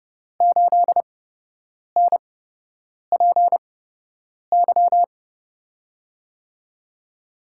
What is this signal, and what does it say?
Morse code, heard as a keyed tone, about 30 wpm, spaced as for 8 wpm, 700 Hz. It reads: 8NPY
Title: USA Call Signs - 6 Characters - 8x Character Spacing - 30wpm